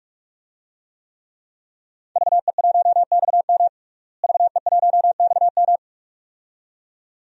Morse code, heard as a keyed tone, about 47 wpm, 700 Hz. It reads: VE1XM VE1XM
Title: DXCC-100 Call Signs - 5 Characters - Repeated 2x - 45wpm